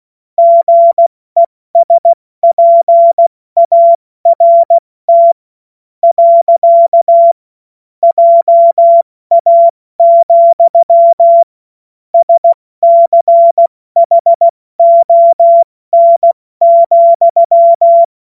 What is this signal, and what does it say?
Morse code, heard as a keyed tone, about 16 wpm, spaced as for 13 wpm, 680 Hz. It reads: GESPART. JA, SCHON,
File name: Morsekurs / Witze / Witz 06. April 2021, 70 F80